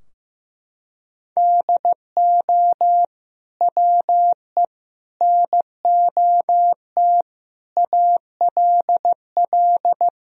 Morse code, heard as a keyed tone, about 15 wpm, 700 Hz. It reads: DO WE NOT ALL
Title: Sentences from Top 1000 Words - Review Part 3 - 15wpm